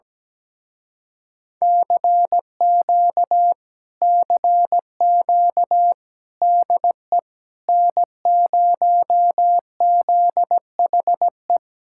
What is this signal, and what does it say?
Morse code, heard as a keyed tone, about 17 wpm, 700 Hz. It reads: CQ CQ DE N0ZHE
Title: Speed-Racing Common QSO Phrases - 17wpm